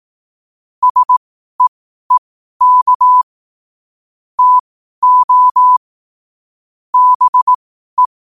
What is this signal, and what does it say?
Morse code, heard as a keyed tone, about 18 wpm, spaced as for 12 wpm, 1000 Hz. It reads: SEEK TO BE
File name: qotdfiles / 12wpm